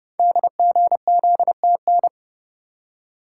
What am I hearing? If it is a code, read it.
Morse code, heard as a keyed tone, about 29 wpm, 700 Hz. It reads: DGZTN
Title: Code Groups - 5 Letter:Numbers - 30wpm